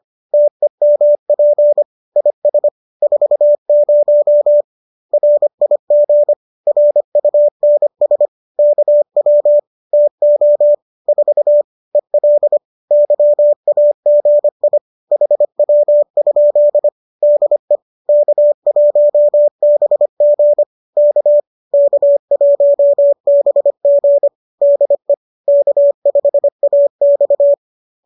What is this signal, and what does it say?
Morse code, heard as a keyed tone, about 25 wpm, 590 Hz. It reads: TEMP IS 40 RIG RUNS KW TO 4 EL YAGI HW? DE K1BG K K1BG DE K5AX